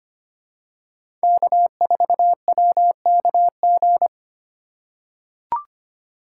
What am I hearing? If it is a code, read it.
Morse code, heard as a keyed tone, about 25 wpm, 700 Hz. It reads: K4WKG
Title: USA Call Signs - 5 Characters - 2x Character Spacing - 25wpm